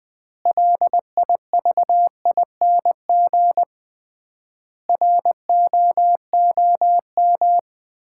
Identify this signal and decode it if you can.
Morse code, heard as a keyed tone, about 21 wpm, 700 Hz. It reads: LIVING ROOM
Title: 3x Word-Spacing - Sets of 4 Words - 20wpm